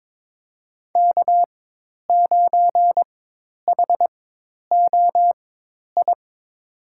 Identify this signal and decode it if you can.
Morse code, heard as a keyed tone, about 23 wpm, spaced as for 10 wpm, 700 Hz. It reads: K9HOI
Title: USA Call Signs - 4x Character Spacing - 22wpm